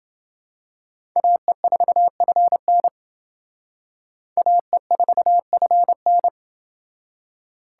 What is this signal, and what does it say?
Morse code, heard as a keyed tone, about 28 wpm, 700 Hz. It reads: AE4FN AE4FN